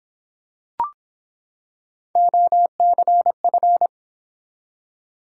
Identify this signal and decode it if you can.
Morse code, heard as a keyed tone, about 25 wpm, 700 Hz. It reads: OCF